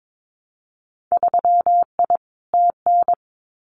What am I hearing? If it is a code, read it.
Morse code, heard as a keyed tone, about 22 wpm, 700 Hz. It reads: 3I TN
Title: Winter Field Day Exchanges 22wpm